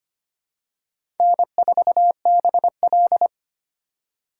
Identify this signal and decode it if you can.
Morse code, heard as a keyed tone, about 25 wpm, 690 Hz. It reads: N4BL
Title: USA Call Signs - 4 Characters - 4x Character Spacing - 25wpm